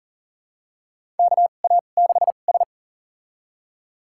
Morse code, heard as a keyed tone, about 40 wpm, spaced as for 29 wpm, 700 Hz. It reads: KA6S
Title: USA Call Signs - 4 Characters - 2x Character Spacing - 40wpm